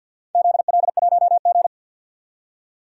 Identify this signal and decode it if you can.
Morse code, heard as a keyed tone, about 51 wpm, 700 Hz. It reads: ZL1G
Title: DXCC-100 Call Signs - 4 Characters - 50wpm